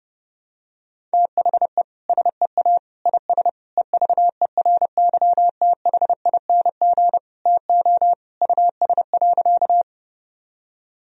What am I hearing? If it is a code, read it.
Morse code, heard as a keyed tone, about 32 wpm, 700 Hz. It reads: THE SEA IS EVERYTHING TO US.
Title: Sentences from Top 1000 Words - Review Part 1 - 30wpm